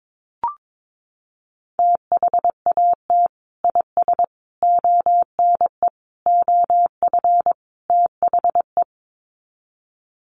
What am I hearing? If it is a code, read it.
Morse code, heard as a keyed tone, about 22 wpm, 700 Hz. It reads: THAT IS ONE OF THE
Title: Sets of 5 Words Encore 22wpm